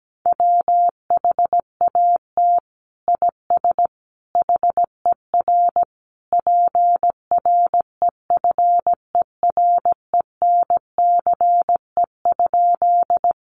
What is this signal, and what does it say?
Morse code, heard as a keyed tone, about 17 wpm, 700 Hz. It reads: WHAT IS HER PREFERENCE?
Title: Sentences from Top 1500 Words 17wpm